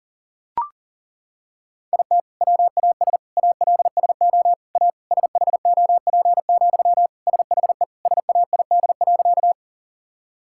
Morse code, heard as a keyed tone, about 40 wpm, 700 Hz. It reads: IT WAS ALSO A SHOP, SHE SAID.
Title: Sentences from Top 1600 Words 40wpm